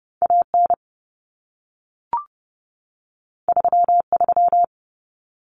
Morse code, heard as a keyed tone, about 30 wpm, 700 Hz. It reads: AN 33
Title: Common Cut Numbers - 2-Digit 30wpm